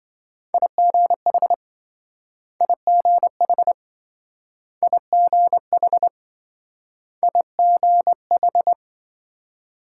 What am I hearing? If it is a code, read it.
Morse code, heard as a keyed tone, about 26 wpm, 700 Hz. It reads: IGH IGH IGH IGH